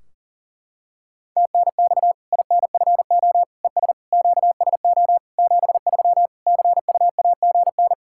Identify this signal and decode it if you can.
Morse code, heard as a keyed tone, about 40 wpm, 700 Hz. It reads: TNX INFO ES QSO 73 CUAGN
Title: K1USN SST QSO Practice 40wpm